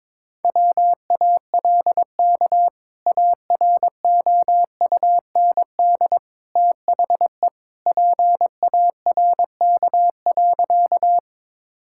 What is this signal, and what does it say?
Morse code, heard as a keyed tone, about 22 wpm, 700 Hz. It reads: WALK AROUND THE PARK.
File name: practice / 22wpm / Sentences with 2-Word Phrases - 22wpm